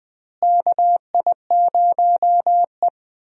Morse code, heard as a keyed tone, about 20 wpm, 700 Hz. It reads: KI0E
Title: USA Call Signs - 4 Characters - Repeated 3x - 20wpm